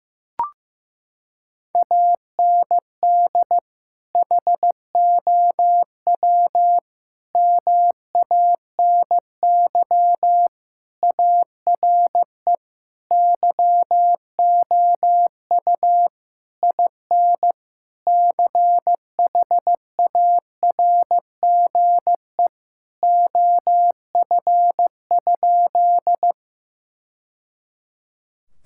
Morse code, heard as a keyed tone, about 15 wpm, 700 Hz. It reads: AND HOW MANY ARE YOU IN CHARGE OF?
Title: Sentences from Top 1000 Words - Review Part 6 - 15wpm